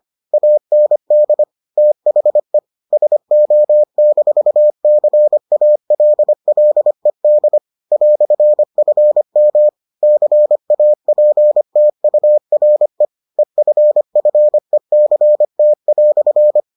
Morse code, heard as a keyed tone, about 25 wpm, 600 Hz. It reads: AND THE SO-CALLED "FM CAPTURE EFFECT"